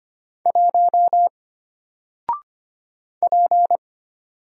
Morse code, heard as a keyed tone, about 25 wpm, 700 Hz. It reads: 1 P
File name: Single-Letter-Number-25wpm